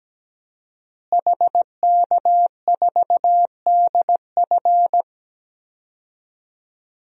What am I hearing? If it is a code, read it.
Morse code, heard as a keyed tone, about 17 wpm, 700 Hz. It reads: HK4DF
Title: DXCC-100 Call Signs - 5 Characters - Repeated 2x - 17wpm